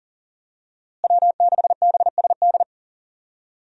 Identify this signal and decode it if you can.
Morse code, heard as a keyed tone, about 40 wpm, 700 Hz. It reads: W6BSD
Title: USA Call Signs - 5 Characters - Repeated 3x - 40wpm